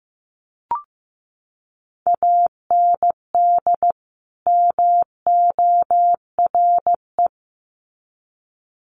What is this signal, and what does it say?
Morse code, heard as a keyed tone, about 15 wpm, 700 Hz. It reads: AND MORE
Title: Sets of 2 Words 15wpm